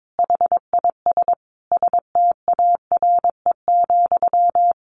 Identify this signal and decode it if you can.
Morse code, heard as a keyed tone, about 22 wpm, 700 Hz. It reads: HIS STARE,
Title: A Sound Of Thunder 22wpm